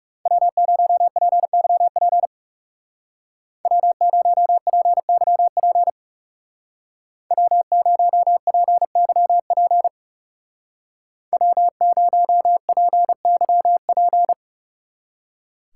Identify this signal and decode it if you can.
Morse code, heard as a keyed tone, about 38 wpm, 700 Hz. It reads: W0PYP W0PYP W0PYP W0PYP